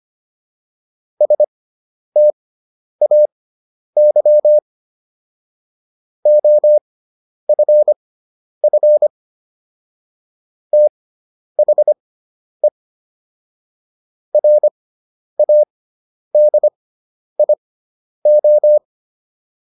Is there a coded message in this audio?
Yes: Morse code from 1.2 to 18.8 s, heard as a keyed tone, about 25 wpm, spaced as for 10 wpm, 600 Hz. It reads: STAY OFF THE RADIO